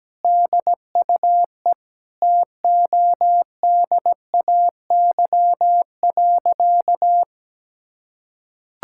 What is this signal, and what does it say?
Morse code, heard as a keyed tone, about 17 wpm, 700 Hz. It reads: DUE TODAY.
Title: Sentences from Top 1000 Words - Review Part 6 - 17wpm